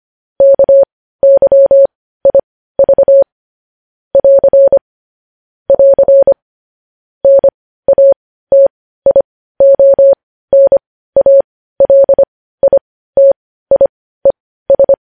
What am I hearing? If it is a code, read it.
Morse code, heard as a keyed tone, about 25 wpm, spaced as for 15 wpm, 550 Hz. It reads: KYIV <AR> <AR> NATIONALITIES
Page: news headlines in morse code at 15 wpm